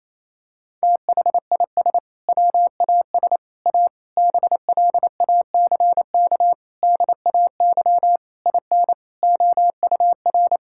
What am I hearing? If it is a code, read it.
Morse code, heard as a keyed tone, about 28 wpm, 700 Hz. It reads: THIS WAS A BLACK DAY IN OUR
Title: Sentences from Top 1600 Words 28wpm